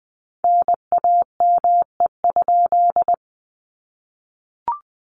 Morse code, heard as a keyed tone, about 20 wpm, 700 Hz. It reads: NAME?
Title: Common QSO Phrases 20wpm